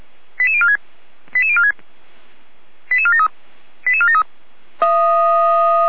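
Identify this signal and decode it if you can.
Five-tone selective call (ZVEI), heard as quick runs of five tones, tones of about 70 ms. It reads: 88046 88046 88473 88473
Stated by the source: ZVEI five-tone call 88245 received by SDR.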